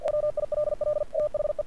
Morse code, heard as a keyed tone, about 48 wpm, 590 Hz. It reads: WILL THE
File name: archived / CRSnet